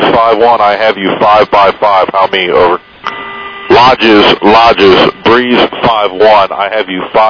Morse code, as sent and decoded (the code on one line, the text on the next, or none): none